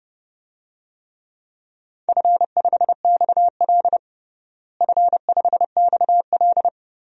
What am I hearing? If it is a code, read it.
Morse code, heard as a keyed tone, about 30 wpm, 700 Hz. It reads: F5XL F5XL
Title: DXCC-100 Call Signs - 4 Characters - Repeated 3x - 30wpm